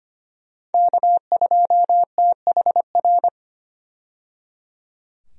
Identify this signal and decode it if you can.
Morse code, heard as a keyed tone, about 25 wpm, 700 Hz. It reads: K2THR
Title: USA Call Signs - 5 Characters - 25wpm